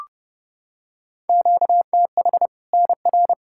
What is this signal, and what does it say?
Morse code, heard as a keyed tone, about 30 wpm, 700 Hz. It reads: QTH NR
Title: Practice QSO - Exchange 1 - Beginning - 30wpm